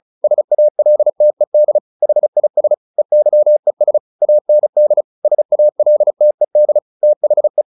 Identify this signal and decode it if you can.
Morse code, heard as a keyed tone, about 34 wpm, 600 Hz. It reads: SALTED HIS EYES AND SALTED THE
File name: the old man and the sea 2 35wpm